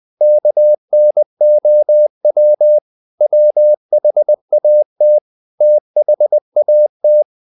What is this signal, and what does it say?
Morse code, heard as a keyed tone, about 20 wpm, 600 Hz. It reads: KNOW WHAT THAT
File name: the old man and the sea / the old man and the sea part 1 20wpm